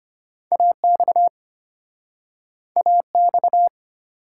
Morse code, heard as a keyed tone, about 34 wpm, 700 Hz. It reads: AX AX